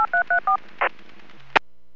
DTMF, heard as telephone touch tones, tones of about 110 ms. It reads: 8334
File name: WPS / 00071 08-02 WPS 451.25 WPS